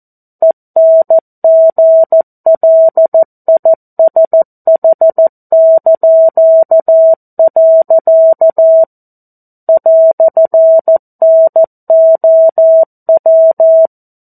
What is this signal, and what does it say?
Morse code, heard as a keyed tone, about 14 wpm, 650 Hz. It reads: ENGLISH). "NOW
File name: 02-Chapter-02 14 WPM